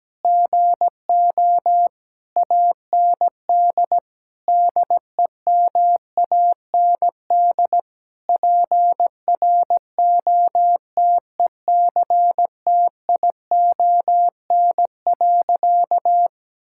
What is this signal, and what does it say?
Morse code, heard as a keyed tone, about 17 wpm, 700 Hz. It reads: GO AND DEMAND PROTECTION.
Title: Sentences from Top 1600 Words 17wpm